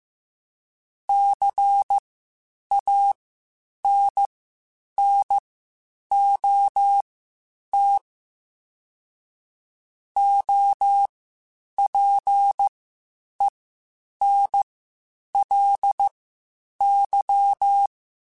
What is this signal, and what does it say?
Morse code, heard as a keyed tone, about 15 wpm, spaced as for 8 wpm, 800 Hz. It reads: CANNOT OPENLY
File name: Quotes Spaced / Quotes Spaced-mp3s / Advice sp @13